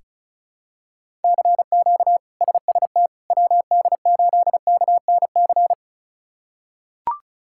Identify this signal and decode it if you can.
Morse code, heard as a keyed tone, about 36 wpm, 700 Hz. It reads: CQ SST WD8KNC
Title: K1USN SST QSO Practice - 35wpm